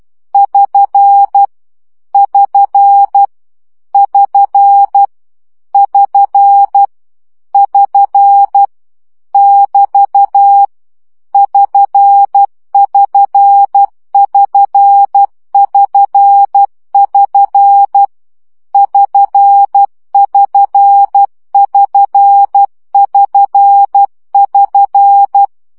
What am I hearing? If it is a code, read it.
Morse code, heard as a keyed tone, about 12 wpm, 800 Hz. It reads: <SN> <SN> <SN> <SN> <SN> <BT> <SN><SN><SN><SN><SN> <SN><SN><SN><SN><SN>
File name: CW gyakorlas sn 10 baud